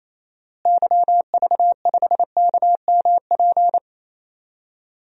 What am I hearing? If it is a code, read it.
Morse code, heard as a keyed tone, about 27 wpm, 700 Hz. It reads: YV5KMP
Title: DXCC-100 Call Signs - 6 Characters - Repeated 2x - 28wpm